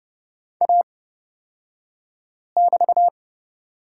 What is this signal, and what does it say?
Morse code, heard as a keyed tone, about 30 wpm, spaced as for 5 wpm, 700 Hz. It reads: A<BT>
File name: Lesson 38d - Letter q - Two Characters 30wpm